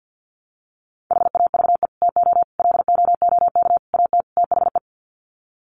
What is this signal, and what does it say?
Morse code, heard as a keyed tone, about 50 wpm, 700 Hz. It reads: HAVE TO LOOK AT THE
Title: Sets of 5 Words Encore 50wpm